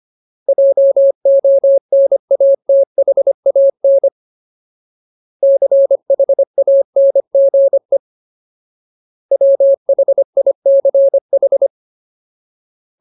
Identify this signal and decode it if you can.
Morse code, heard as a keyed tone, about 25 wpm, 550 Hz. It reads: JONATHAN CHANGE WHICH
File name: Ultimate Word List / 25WPM Ultimate